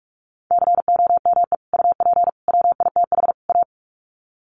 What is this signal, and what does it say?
Morse code, heard as a keyed tone, about 44 wpm, 700 Hz. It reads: COME UP WITH A
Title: Sets of 4 Words 45wpm